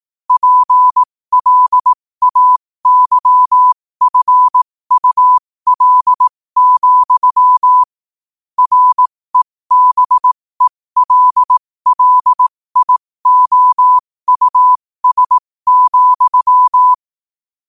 Morse code, heard as a keyed tone, about 18 wpm, spaced as for 15 wpm, 1000 Hz. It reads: PLAYFUL, REBELLIOUS,
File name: qotdfiles / 15wpm